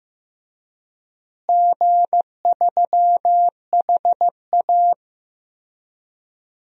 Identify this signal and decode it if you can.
Morse code, heard as a keyed tone, about 15 wpm, 700 Hz. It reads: G3HA
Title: DXCC-100 Call Signs - 4 Characters - Repeated 3x - 15wpm